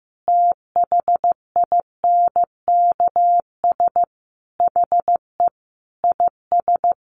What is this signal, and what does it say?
Morse code, heard as a keyed tone, about 15 wpm, 700 Hz. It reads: THINKS HE IS
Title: Additional Easy Phrases 15wpm